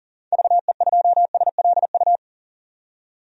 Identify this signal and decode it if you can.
Morse code, heard as a keyed tone, about 40 wpm, 700 Hz. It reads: VE2SLU
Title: DXCC-100 Call Signs - 6 Characters - 40wpm